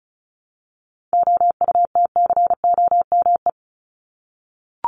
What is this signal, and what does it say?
Morse code, heard as a keyed tone, about 34 wpm, 700 Hz. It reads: OUTCOME
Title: Top 1501-1750 Words 35wpm